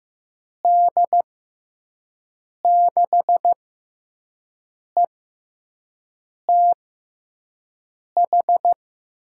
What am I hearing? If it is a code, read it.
Morse code, heard as a keyed tone, about 15 wpm, spaced as for 5 wpm, 700 Hz. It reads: D6ETH
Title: USA Call Signs - 6 Characters - 6x Character Spacing - 15wpm